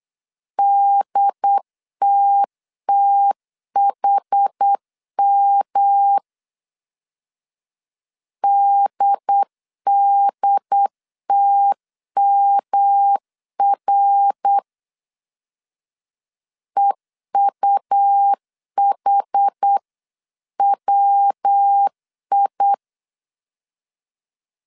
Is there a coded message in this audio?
Yes: Morse code from 0.6 to 22.7 s, heard as a keyed tone, about 8 wpm, 800 Hz. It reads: DTTHM DDTMR EUHWI